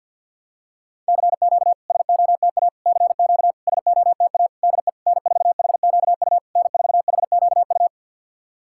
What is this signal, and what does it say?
Morse code, heard as a keyed tone, about 50 wpm, 690 Hz. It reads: CQ SOTA CQ SOTA DE N4HQU N4HQU
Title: SOTA QSO Practice 50wpm